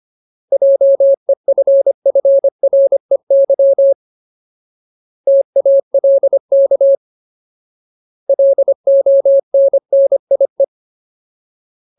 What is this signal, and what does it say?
Morse code, heard as a keyed tone, about 25 wpm, 550 Hz. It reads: JEFFREY TALK LONNIE
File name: Ultimate Word List / 25WPM Ultimate